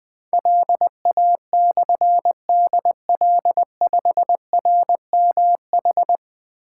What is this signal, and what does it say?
Morse code, heard as a keyed tone, about 20 wpm, 700 Hz. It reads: LA/DL5RMH
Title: Difficult Call Signs 20wpm